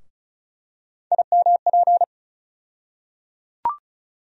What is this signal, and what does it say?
Morse code, heard as a keyed tone, about 34 wpm, 700 Hz. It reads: IMP
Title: USA Call Sign Postfixes - 3 Letters 35wpm